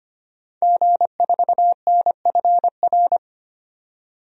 Morse code, heard as a keyed tone, about 25 wpm, 700 Hz. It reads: G4NFR
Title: DXCC-100 Call Signs - 5 Characters - 25wpm